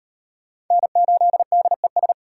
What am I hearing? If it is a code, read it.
Morse code, heard as a keyed tone, about 36 wpm, 700 Hz. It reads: N8DES